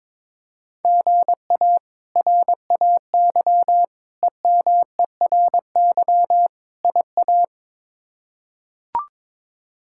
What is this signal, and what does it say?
Morse code, heard as a keyed tone, about 22 wpm, 700 Hz. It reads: GA RAY EMERY IA
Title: K1USN SST QSO Practice - 22wpm